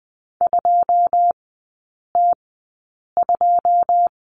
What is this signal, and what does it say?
Morse code, heard as a keyed tone, about 20 wpm, spaced as for 8 wpm, 690 Hz. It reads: 2T2
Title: 081 Lesson 14e - Number 2 - Three Characters 20wpm